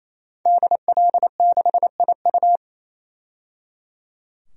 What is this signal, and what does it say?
Morse code, heard as a keyed tone, about 28 wpm, 690 Hz. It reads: DL6IU
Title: DXCC-100 Call Signs - 5 Characters - 28wpm